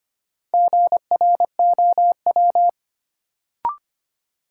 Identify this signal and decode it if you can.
Morse code, heard as a keyed tone, about 25 wpm, 700 Hz. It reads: GROW
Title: Sentences from Top 300 Words - New Word Drill - 25wpm